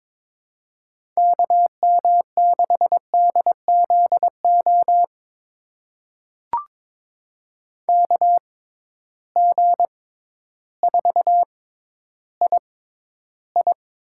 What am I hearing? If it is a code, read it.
Morse code, heard as a keyed tone, about 22 wpm, spaced as for 12 wpm, 700 Hz. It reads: KM6DZO KG4II
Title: USA Call Signs - 6 Characters - 6x Character Spacing - 22wpm